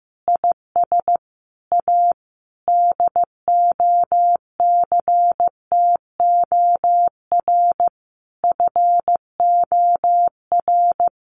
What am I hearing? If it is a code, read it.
Morse code, heard as a keyed tone, about 15 wpm, 700 Hz. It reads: IS A DOCTOR FOR
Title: Additional Easy Phrases 15wpm